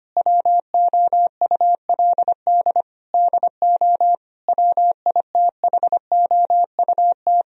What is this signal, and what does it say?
Morse code, heard as a keyed tone, about 25 wpm, 700 Hz. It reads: WOULD DO WITHOUT
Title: Sentences from Top 1000 Words - Review Part 3 - 25wpm